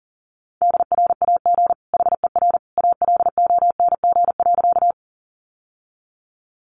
Morse code, heard as a keyed tone, about 40 wpm, 700 Hz. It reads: DRAG HER ALONG.
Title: Sentences from Top 1500 Words 40wpm